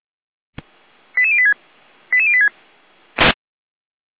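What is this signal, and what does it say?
Five-tone selective call (ZVEI), heard as quick runs of five tones, tones of about 70 ms. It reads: 88086 88086
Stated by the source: ZVEI five-tone selective call received by SDR.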